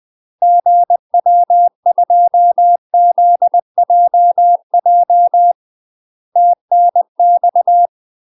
Morse code, heard as a keyed tone, about 20 wpm, 700 Hz. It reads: GW2ZJJ TNX
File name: UK MASTER QSOs TRAINING/20WPM